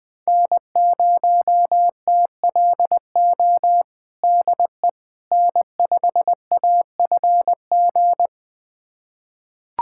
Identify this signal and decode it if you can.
Morse code, heard as a keyed tone, about 20 wpm, 700 Hz. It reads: N0TLO DE N5AFG
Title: USA Callsign DE Callsign - 5 Characters - 20wpm